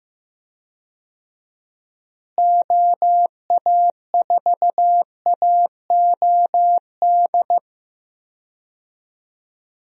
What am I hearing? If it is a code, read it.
Morse code, heard as a keyed tone, about 15 wpm, 700 Hz. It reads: OA4AOD
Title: USA Call Signs - 6 Characters - Repeated 2x - 15wpm